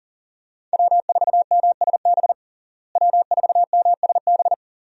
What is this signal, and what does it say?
Morse code, heard as a keyed tone, about 40 wpm, 700 Hz. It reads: W4MSB W4MSB
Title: USA Call Signs - 5 Characters - Repeated 2x - 40wpm